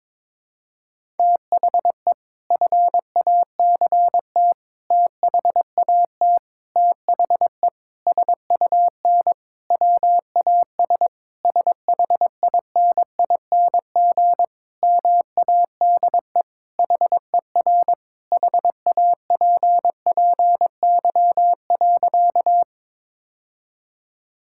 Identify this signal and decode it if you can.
Morse code, heard as a keyed tone, about 22 wpm, 700 Hz. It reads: THE FACT THAT THE SUN WAS SHINING MADE HER HAPPY.
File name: practice / 22wpm / Sentences with 4-Word Phrases - 22wpm